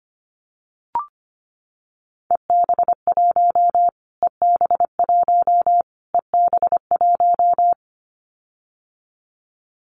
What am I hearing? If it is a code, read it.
Morse code, heard as a keyed tone, about 25 wpm, 700 Hz. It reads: EB1 EB1 EB1
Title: Callsign Prefixes for Top 25 DXCC - Repeated 3x - 25wpm